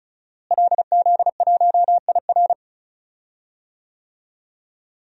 Morse code, heard as a keyed tone, about 34 wpm, 700 Hz. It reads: LZ1IR
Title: DXCC-100 Call Signs - 5 Characters - Repeated 3x - 35wpm